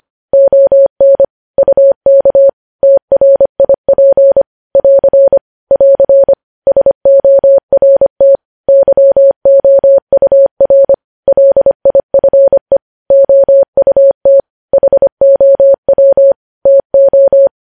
Morse code, heard as a keyed tone, about 25 wpm, 550 Hz. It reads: ON UK TRIP <AR> <AR> SORT YOUR LIFE OUT HOW TO
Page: news-headlines-in-morse-code-at-25-wpm